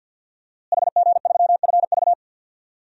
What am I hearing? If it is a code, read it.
Morse code, heard as a keyed tone, about 51 wpm, 700 Hz. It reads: HG3FV
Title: DXCC-100 Call Signs - 5 Characters - Repeated 3x - 50wpm